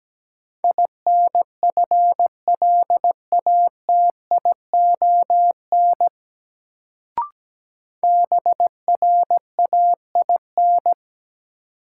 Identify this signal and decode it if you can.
Morse code, heard as a keyed tone, about 17 wpm, 700 Hz. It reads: INFLATION BRAIN
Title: Sentences from Top 1100 Words - New Word Drill Alt - 17wpm